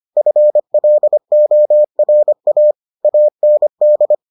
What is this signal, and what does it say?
Morse code, heard as a keyed tone, about 25 wpm, 600 Hz. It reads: FLORA AND